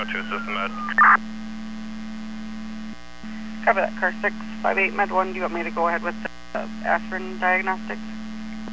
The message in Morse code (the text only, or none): none